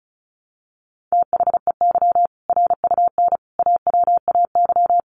Morse code, heard as a keyed tone, about 34 wpm, 700 Hz. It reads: THEY RUN AWAY
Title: Easy Phrases 35wpm